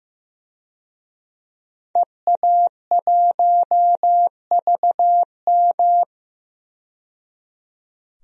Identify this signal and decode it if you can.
Morse code, heard as a keyed tone, about 15 wpm, 700 Hz. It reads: EA1VM
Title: DXCC-100 Call Signs - 5 Characters - 15wpm